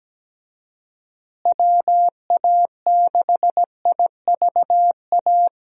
Morse code, heard as a keyed tone, about 17 wpm, 700 Hz. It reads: WA6IVA
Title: USA Call Signs - 6 Characters - Repeated 3x - 17wpm